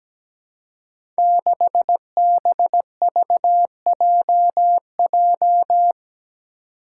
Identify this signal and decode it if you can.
Morse code, heard as a keyed tone, about 17 wpm, 700 Hz. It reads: 6BVJJ
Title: Five Challenging Letter-Number Combinations 17wpm